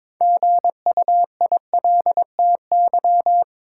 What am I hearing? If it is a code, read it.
Morse code, heard as a keyed tone, about 22 wpm, 700 Hz. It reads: GUILTY
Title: Six Letter Words - 22wpm